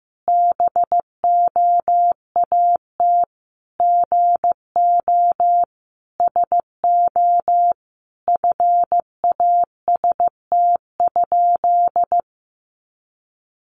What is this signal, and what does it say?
Morse code, heard as a keyed tone, about 14 wpm, 700 Hz. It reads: BOAT GO SO FAST?
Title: Sentences from Top 500 Words 15wpm